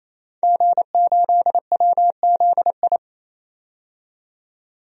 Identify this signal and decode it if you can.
Morse code, heard as a keyed tone, about 28 wpm, 700 Hz. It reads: G8WZI
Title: DXCC-100 Call Signs - 5 Characters - 28wpm